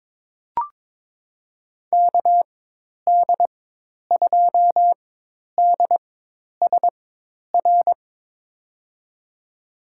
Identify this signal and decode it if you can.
Morse code, heard as a keyed tone, about 22 wpm, spaced as for 10 wpm, 700 Hz. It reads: KD2DSR
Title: USA Call Signs - 6 Characters - 4x Character Spacing - 22wpm